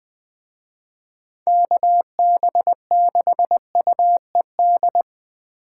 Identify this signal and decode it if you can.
Morse code, heard as a keyed tone, about 24 wpm, 700 Hz. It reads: KB6UED
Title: Speed Racing - USA 2x3 Call Signs 20wpm